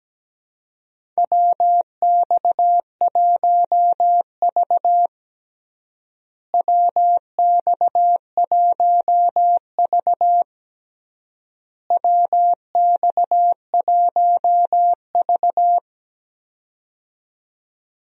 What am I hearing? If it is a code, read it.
Morse code, heard as a keyed tone, about 17 wpm, 700 Hz. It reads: WX1V WX1V WX1V
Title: USA Call Signs - 4 Characters - Repeated 3x - 17wpm